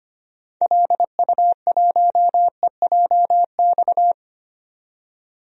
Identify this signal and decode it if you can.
Morse code, heard as a keyed tone, about 25 wpm, 700 Hz. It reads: LU1EJX
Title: DXCC-100 Call Signs - 6 Characters - 25wpm